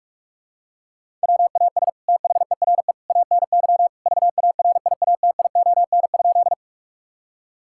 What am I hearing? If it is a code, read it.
Morse code, heard as a keyed tone, about 46 wpm, 700 Hz. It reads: WAS THERE ANY VARIATION?
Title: Sentences from Top 1600 Words 45wpm